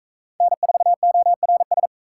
Morse code, heard as a keyed tone, about 44 wpm, 700 Hz. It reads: N4ORS